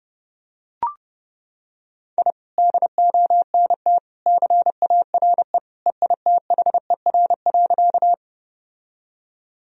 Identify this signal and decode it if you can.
Morse code, heard as a keyed tone, about 30 wpm, 700 Hz. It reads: I DONT CARE EITHER.
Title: Sentences from Top 1000 Words - Review Part 4 - 30wpm